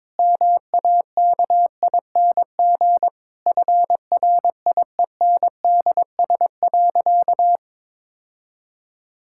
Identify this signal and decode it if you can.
Morse code, heard as a keyed tone, about 22 wpm, 700 Hz. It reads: MAKING FRIENDS.
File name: Difficult Sentences with Hints - 22wpm